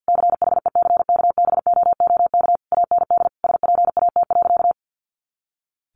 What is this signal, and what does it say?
Morse code, heard as a keyed tone, about 50 wpm, 700 Hz. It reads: CHECKBOOK AND SPAT.
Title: A Sound Of Thunder 50wpm